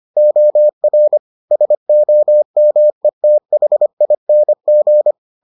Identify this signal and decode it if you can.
Morse code, heard as a keyed tone, about 25 wpm, 600 Hz. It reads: OR SOMETHING